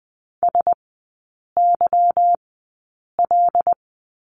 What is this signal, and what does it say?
Morse code, heard as a keyed tone, about 20 wpm, spaced as for 8 wpm, 700 Hz. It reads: SYL
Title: 152 Lesson 25e - Letter p - Three Characters 20wpm